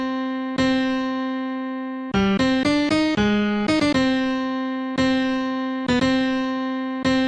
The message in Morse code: none